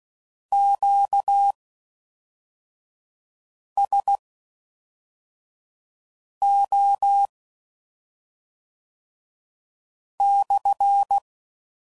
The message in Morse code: QSO/

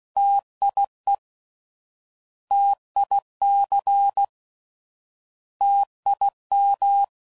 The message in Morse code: TIE TIC TIM

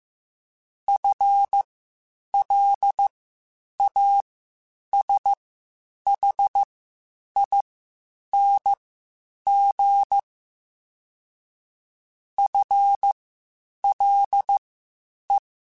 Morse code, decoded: FLASHING FLE

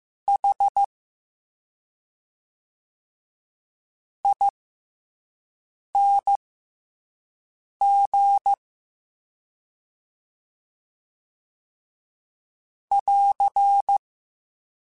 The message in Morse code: H ING <AR>

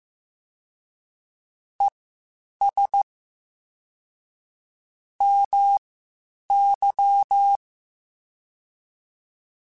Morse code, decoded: ES MY

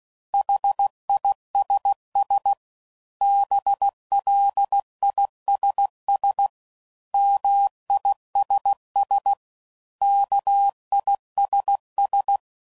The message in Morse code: HISS BLISS MISS KISS

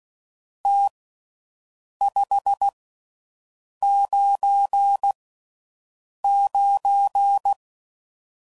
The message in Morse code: T 599

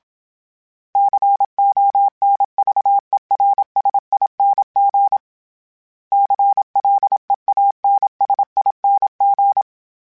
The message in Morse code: CONVERSING CLEANSING

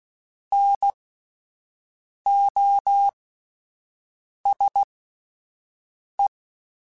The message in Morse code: NOSE

